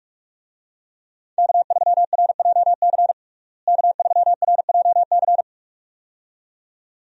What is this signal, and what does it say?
Morse code, heard as a keyed tone, about 45 wpm, 700 Hz. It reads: K3RJC K3RJC
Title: USA Call Signs - 5 Characters - Repeated 2x - 45wpm